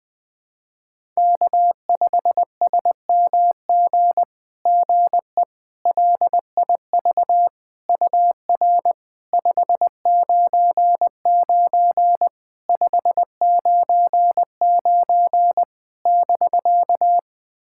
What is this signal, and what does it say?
Morse code, heard as a keyed tone, about 20 wpm, 700 Hz. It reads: K5SMG GE LIV UR 599 599 <BK>
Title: SOTA QSO Practice 20wpm